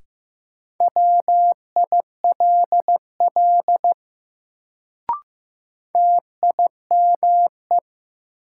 Morse code, heard as a keyed tone, about 16 wpm, 700 Hz. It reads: WILL TIME
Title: Sets of 2 Words - New Word Drill - 15wpm